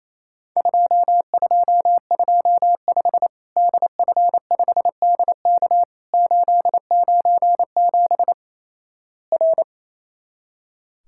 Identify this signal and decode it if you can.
Morse code, heard as a keyed tone, about 28 wpm, 700 Hz. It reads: 2225 DF5DK 897 R
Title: Worked All Europe DX Contest - QTC Exchange - 28wpm